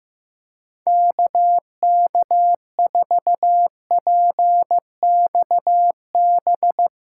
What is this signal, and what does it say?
Morse code, heard as a keyed tone, about 15 wpm, 700 Hz. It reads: KK4PXB
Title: USA Callsign DE Callsign - 6 Characters - 15wpm